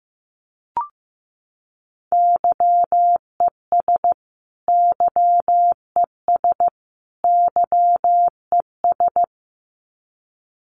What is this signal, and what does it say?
Morse code, heard as a keyed tone, about 15 wpm, 700 Hz. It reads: YES YES YES
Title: Top 500 Words Repeated 15wpm